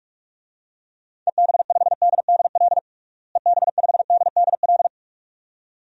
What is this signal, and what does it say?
Morse code, heard as a keyed tone, about 41 wpm, 700 Hz. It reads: EB5DDL EB5DDL
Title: DXCC-100 Call Signs - 6 Characters - Repeated 2x - 45wpm